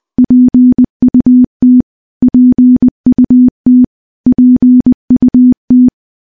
Morse code, heard as a keyed tone, about 20 wpm, 250 Hz. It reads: PUT PUT PUT
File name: morse-training-20WPM